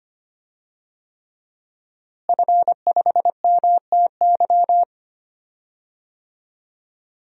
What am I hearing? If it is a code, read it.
Morse code, heard as a keyed tone, about 25 wpm, 700 Hz. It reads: F5MTY
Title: DXCC-100 Call Signs - 5 Characters - Repeated 3x - 25wpm